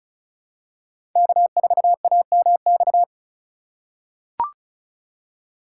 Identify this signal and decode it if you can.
Morse code, heard as a keyed tone, about 36 wpm, 700 Hz. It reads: K4AMX
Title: USA Call Signs - 5 Characters - 4x Character Spacing - 35wpm